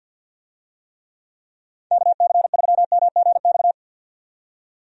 Morse code, heard as a keyed tone, about 51 wpm, 700 Hz. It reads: KC3MGX
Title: USA Call Signs - 6 Characters - 50wpm